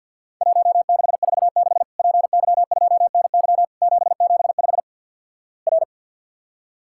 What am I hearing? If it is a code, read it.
Morse code, heard as a keyed tone, about 50 wpm, 700 Hz. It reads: 1646 PY2NY 775 R
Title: Worked All Europe DX Contest - QTC Exchange - 50wpm